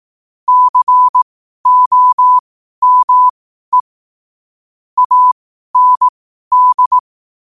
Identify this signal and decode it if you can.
Morse code, heard as a keyed tone, about 18 wpm, spaced as for 12 wpm, 1000 Hz. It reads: COME AND
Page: New quotes every day in morse code at 12 Words per minute.